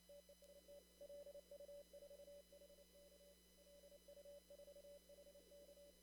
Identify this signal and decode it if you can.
Morse code, heard as a keyed tone, about 29 wpm, 570 Hz. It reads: TEST LU4HK LU4HK
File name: CW / kt5j wpx cw-op-k5pi